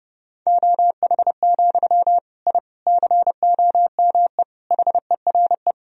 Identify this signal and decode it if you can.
Morse code, heard as a keyed tone, about 30 wpm, 700 Hz. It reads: OH, I COME HERE
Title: Sentences from Top 1000 Words - Review Part 2 - 30wpm